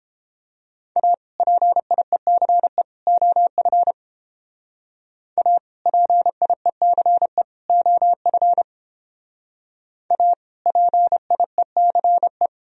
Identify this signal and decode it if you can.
Morse code, heard as a keyed tone, about 30 wpm, 700 Hz. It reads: A PIECE OF A PIECE OF A PIECE